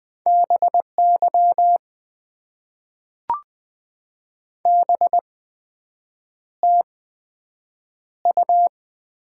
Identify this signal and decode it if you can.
Morse code, heard as a keyed tone, about 20 wpm, spaced as for 7 wpm, 700 Hz. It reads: BY BTU
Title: Top 90 QSO Elements - 8x Character Spacing - 20wpm